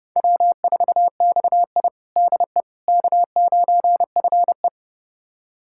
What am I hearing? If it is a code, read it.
Morse code, heard as a keyed tone, about 30 wpm, 700 Hz. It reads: W4XI DE K9FE
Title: USA Callsign DE Callsign - 4 Characters - 30wpm